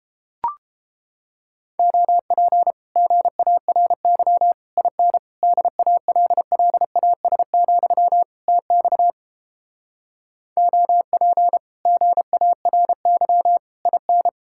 Morse code, heard as a keyed tone, about 29 wpm, 700 Hz. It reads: OP GARY IN DALLAS, TX OP GARY IN